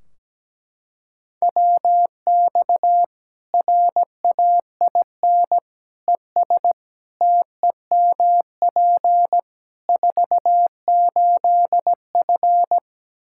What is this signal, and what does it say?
Morse code, heard as a keyed tone, about 17 wpm, 700 Hz. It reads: WX RAIN ES TEMP 48F
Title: Practice QSO - Exchange 2 - Info - 17wpm